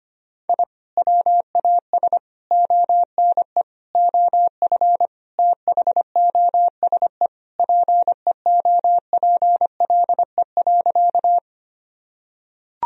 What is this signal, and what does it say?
Morse code, heard as a keyed tone, about 25 wpm, 700 Hz. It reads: I WAS ONE OF THOSE PEOPLE.
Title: Sentences from Top 1000 Words - Review Part 1 - 25wpm